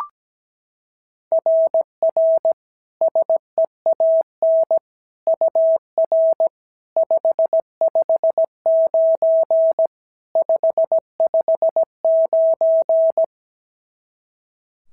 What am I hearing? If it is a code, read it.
Morse code, heard as a keyed tone, about 15 wpm, 650 Hz. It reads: RR SEAN UR 559 559
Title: SOTA QSO Practice 17wpm